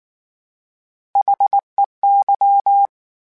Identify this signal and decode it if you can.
Morse code, heard as a keyed tone, about 19 wpm, 800 Hz. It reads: HEY
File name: H Prefixes / H mp3s